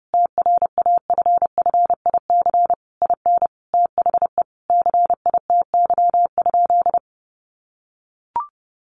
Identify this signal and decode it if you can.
Morse code, heard as a keyed tone, about 30 wpm, 700 Hz. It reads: TRAFFIC IN THE CITY?
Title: Sentences from Top 1300 Words 30wpm